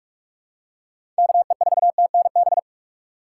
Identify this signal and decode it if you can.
Morse code, heard as a keyed tone, about 46 wpm, 700 Hz. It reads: KE4TNB
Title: USA Call Signs - 6 Characters - 4x Character Spacing - 45wpm